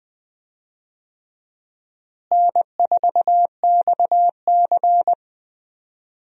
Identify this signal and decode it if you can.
Morse code, heard as a keyed tone, about 21 wpm, 700 Hz. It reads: N4XC
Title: USA Call Signs - 4 Characters - Repeated 3x - 20wpm